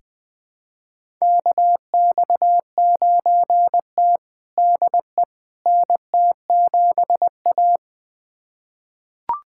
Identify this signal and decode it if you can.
Morse code, heard as a keyed tone, about 20 wpm, 700 Hz. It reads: KX9T DE NT7A
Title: USA Callsign DE Callsign - 4 Characters - 20wpm